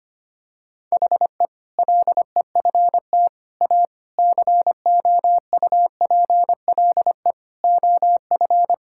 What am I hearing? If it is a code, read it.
Morse code, heard as a keyed tone, about 25 wpm, 700 Hz. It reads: HE LEFT A COUPLE OF
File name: practice / 25wpm / Sentences with 4-Word Phrases - 25wpm